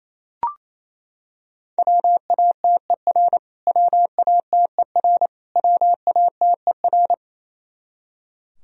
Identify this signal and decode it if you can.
Morse code, heard as a keyed tone, about 28 wpm, 700 Hz. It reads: WATER WATER WATER
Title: Top 100 Words Repeated 28wpm